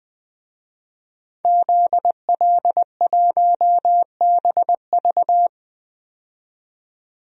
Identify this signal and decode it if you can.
Morse code, heard as a keyed tone, about 21 wpm, 700 Hz. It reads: ZL1BV
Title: DXCC-100 Call Signs - 5 Characters - 20wpm